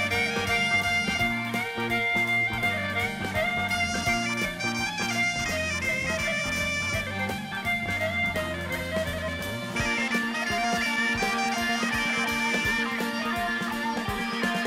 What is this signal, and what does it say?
Dial-up modem handshake.